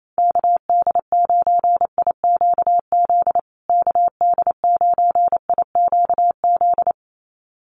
Morse code, heard as a keyed tone, about 28 wpm, 700 Hz. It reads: KD9IQZ KD9IQZ
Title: USA Call Signs Repeated 2x 28wpm